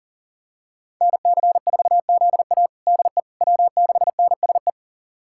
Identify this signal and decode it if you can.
Morse code, heard as a keyed tone, about 40 wpm, 700 Hz. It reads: NC4ZA DE W6NSE
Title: USA Callsign DE Callsign - 5 Characters - 40wpm